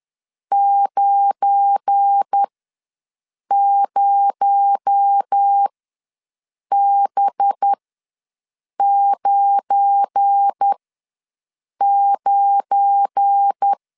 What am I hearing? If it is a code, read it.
Morse code, heard as a keyed tone, about 10 wpm, spaced as for 6 wpm, 800 Hz. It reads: 90B99
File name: LEZIONI / 7 GF9Z / 7 GF9 ws 10wpm 4034